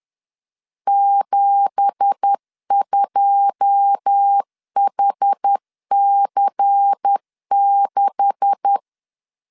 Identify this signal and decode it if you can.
Morse code, heard as a keyed tone, about 10 wpm, 800 Hz. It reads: 72HC6
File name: LEZIONI / 10 SIGNS / 10D COMA 10wpm 3013